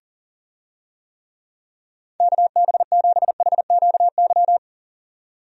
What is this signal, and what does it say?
Morse code, heard as a keyed tone, about 38 wpm, 700 Hz. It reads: KB7HQY
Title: USA Call Signs - 6 Characters - 40wpm